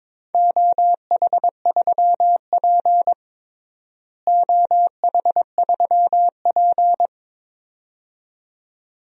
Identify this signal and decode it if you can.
Morse code, heard as a keyed tone, about 22 wpm, 700 Hz. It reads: OH3P OH3P
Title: DXCC-100 Call Signs - 4 Characters - Repeated 2x - 22wpm